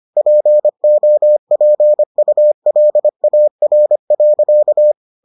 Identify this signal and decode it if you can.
Morse code, heard as a keyed tone, about 25 wpm, 600 Hz. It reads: POPULAR.